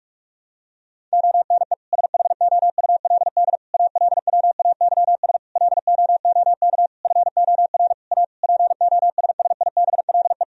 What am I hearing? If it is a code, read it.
Morse code, heard as a keyed tone, about 46 wpm, 700 Hz. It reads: ONE SHOULD ALWAYS LOOK FOR A POSSIBLE
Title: Sentences from Top 1000 Words - Review Part 4 - 45wpm